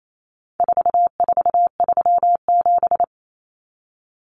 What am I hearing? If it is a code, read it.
Morse code, heard as a keyed tone, about 28 wpm, 700 Hz. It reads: 4437
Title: Numbers - 4-Digit 28wpm